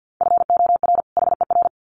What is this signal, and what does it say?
Morse code, heard as a keyed tone, about 50 wpm, 700 Hz. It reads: FOR HER
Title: Sets of 2 Words 50wpm